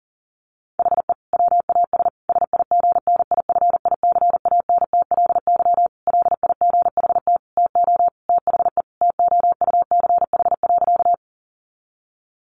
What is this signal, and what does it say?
Morse code, heard as a keyed tone, about 40 wpm, 700 Hz. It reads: HE WAS SIGNIFICANTLY LIGHT TO THE TOUCH.
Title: Sentences from Top 1500 Words 40wpm